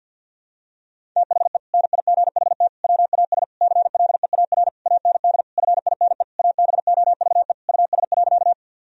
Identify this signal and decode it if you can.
Morse code, heard as a keyed tone, about 52 wpm, 700 Hz. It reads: THE NIGHT WAS CLEAR AND FINE ABOVE US.
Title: Sentences from Top 1000 Words - Review Part 2 - 50wpm